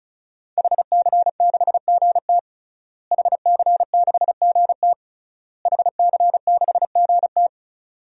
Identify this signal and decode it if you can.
Morse code, heard as a keyed tone, about 34 wpm, 700 Hz. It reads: HC6GT HC6GT HC6GT
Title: DXCC-100 Call Signs - 5 Characters - Repeated 3x - 35wpm